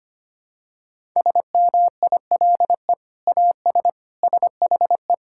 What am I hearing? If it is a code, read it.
Morse code, heard as a keyed tone, about 25 wpm, 700 Hz. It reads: SMILE AS SHE
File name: practice / 25wpm / Sentences with 2-Word Phrases - 25wpm